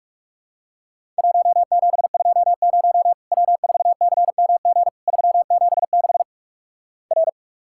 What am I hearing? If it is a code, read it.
Morse code, heard as a keyed tone, about 44 wpm, 700 Hz. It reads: 1720 W4CMG 376 R
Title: Worked All Europe DX Contest - QTC Exchange - 45wpm